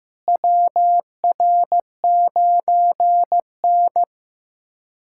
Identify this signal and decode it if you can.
Morse code, heard as a keyed tone, about 16 wpm, 700 Hz. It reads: WR9N
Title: USA Callsign DE Callsign - 4 Characters - 15wpm